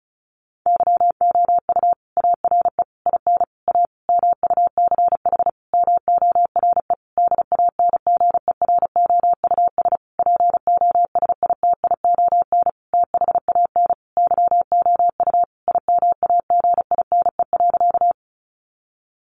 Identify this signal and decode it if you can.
Morse code, heard as a keyed tone, about 34 wpm, 700 Hz. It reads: YOU ARE IN A MUCH MORE DANGEROUS POSITION THAN YOU IMAGINE.
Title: Sentences from Top 1400 Words 35wpm